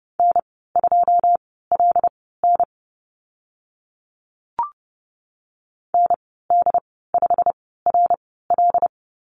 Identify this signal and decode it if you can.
Morse code, heard as a keyed tone, about 30 wpm, spaced as for 17 wpm, 690 Hz. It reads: N2LN ND5RL
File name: Lesson 15g - Number 5 - Call Signs 30wpm